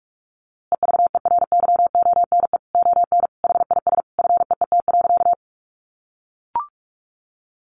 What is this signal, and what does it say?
Morse code, heard as a keyed tone, about 44 wpm, 700 Hz. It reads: EVERYONE ON HIS FEET.
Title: A Sound Of Thunder 45wpm